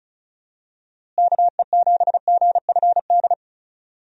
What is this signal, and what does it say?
Morse code, heard as a keyed tone, about 36 wpm, 700 Hz. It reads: KE7GFD
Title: USA Call Signs - 6 Characters - Repeated 3x - 35wpm